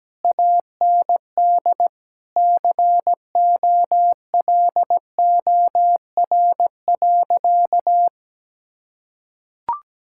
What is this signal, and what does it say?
Morse code, heard as a keyed tone, about 17 wpm, 700 Hz. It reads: AND COLOR.
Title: Sentences from Top 1600 Words 17wpm